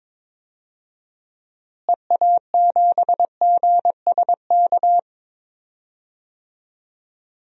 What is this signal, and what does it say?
Morse code, heard as a keyed tone, about 22 wpm, 700 Hz. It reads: EA7GSK
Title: DXCC-100 Call Signs - 6 Characters - Repeated 3x - 22wpm